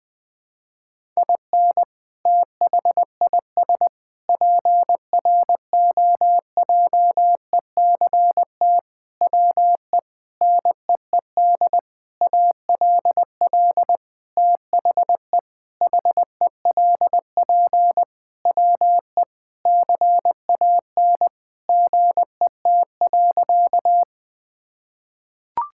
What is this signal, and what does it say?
Morse code, heard as a keyed tone, about 20 wpm, 700 Hz. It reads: IN THIS PROJECT WE NEED ALL THE HELP WE CAN GET.
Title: Sentences with 8-Word Phrases - 20wpm